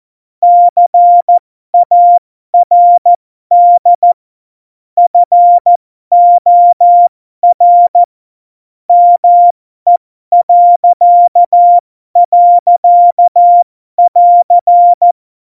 Morse code, heard as a keyed tone, about 14 wpm, spaced as for 12 wpm, 700 Hz. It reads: CARD FOR ME..<AR>
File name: Morsekurs / Witze